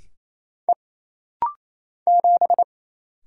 Morse code, heard as a keyed tone, about 27 wpm, 700 Hz. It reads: E 7
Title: Single Letter-Number - Rapid-Fire 28wpm